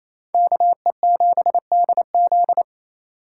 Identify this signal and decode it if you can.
Morse code, heard as a keyed tone, about 35 wpm, 700 Hz. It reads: KE7DZ
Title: Speed Racing - USA 2x2 Call Signs 28wpm